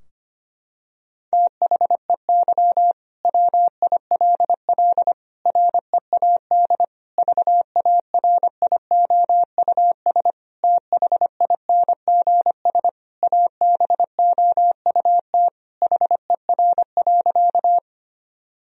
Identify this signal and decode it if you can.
Morse code, heard as a keyed tone, about 25 wpm, 700 Hz. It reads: THEY WILL READ VARIOUS THINGS ABOUT HER.
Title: Sentences from Top 1000 Words - Review Part 4 - 25wpm